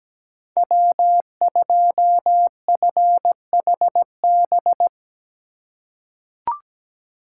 Morse code, heard as a keyed tone, about 17 wpm, 700 Hz. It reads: W2FHB
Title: USA Call Signs - 8x Character Spacing - 17wpm